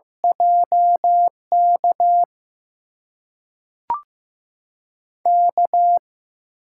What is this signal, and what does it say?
Morse code, heard as a keyed tone, about 15 wpm, spaced as for 3 wpm, 700 Hz. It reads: J KK
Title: USA Call Signs - 6 Characters - 8x Character Spacing - 15wpm